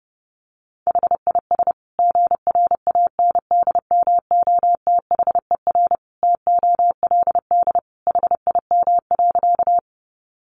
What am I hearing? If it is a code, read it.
Morse code, heard as a keyed tone, about 30 wpm, 700 Hz. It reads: HIS GRANDMOTHER TOLD HIM.
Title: Sentences from Top 1200 Words 30wpm